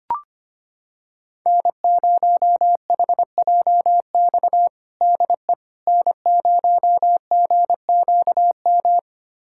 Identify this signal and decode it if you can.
Morse code, heard as a keyed tone, about 25 wpm, 700 Hz. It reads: N0HJX DE N0GQM
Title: USA Callsign DE Callsign - 5 Characters - 25wpm